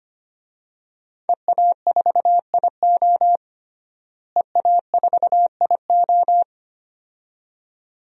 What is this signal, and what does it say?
Morse code, heard as a keyed tone, about 25 wpm, 700 Hz. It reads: EA4IO EA4IO
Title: DXCC-100 Call Signs - 5 Characters - Repeated 2x - 25wpm